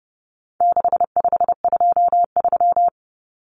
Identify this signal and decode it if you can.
Morse code, heard as a keyed tone, about 30 wpm, 700 Hz. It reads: 6523
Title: Numbers - 4-Digit 30wpm